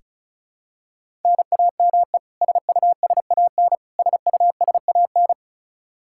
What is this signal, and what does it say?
Morse code, heard as a keyed tone, about 36 wpm, 700 Hz. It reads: NAME SUSAN SUSAN
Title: Practice QSO - Exchange 1 - Beginning - 35wpm